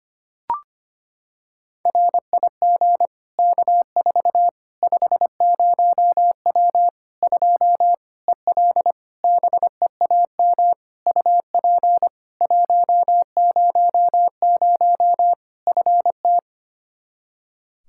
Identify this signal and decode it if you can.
Morse code, heard as a keyed tone, about 25 wpm, 700 Hz. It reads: RIG K4 50W 2 EL BEAM UP 100 FT
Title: Practice QSO - Exchange 2 - Info - 25wpm